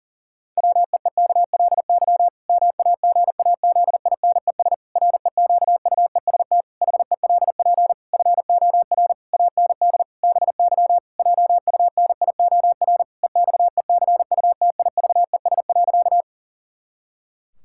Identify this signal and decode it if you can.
Morse code, heard as a keyed tone, about 40 wpm, 700 Hz. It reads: WEEKLY MAGAZINES REQUEST HELP FOR AND BY JUNIOR EXECUTIVES.
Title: Pangram Challenge v2 - 40wpm